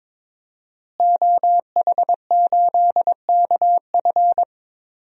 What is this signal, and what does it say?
Morse code, heard as a keyed tone, about 21 wpm, 700 Hz. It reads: OH8KF
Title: DXCC-100 Call Signs - 5 Characters - Repeated 3x - 22wpm